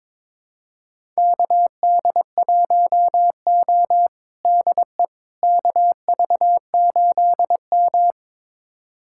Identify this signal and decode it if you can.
Morse code, heard as a keyed tone, about 22 wpm, 700 Hz. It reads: KD1O DE KV8M
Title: USA Callsign DE Callsign - 4 Characters - 22wpm